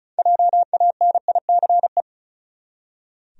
Morse code, heard as a keyed tone, about 36 wpm, 700 Hz. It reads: JANICE
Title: USA Common Names - 35wpm